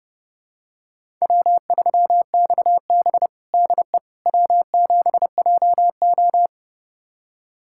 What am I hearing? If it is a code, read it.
Morse code, heard as a keyed tone, about 29 wpm, 700 Hz. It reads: W3XB DE W7JO
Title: USA Callsign DE Callsign - 4 Characters - 30wpm